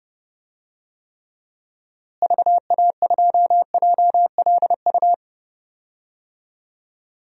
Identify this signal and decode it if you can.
Morse code, heard as a keyed tone, about 30 wpm, 700 Hz. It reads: VA2JLU
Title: DXCC-100 Call Signs - 6 Characters - Repeated 3x - 30wpm